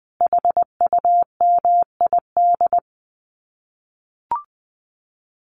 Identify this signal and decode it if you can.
Morse code, heard as a keyed tone, about 20 wpm, 700 Hz. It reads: HUMID
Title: Common QSO Phrases Encore 20wpm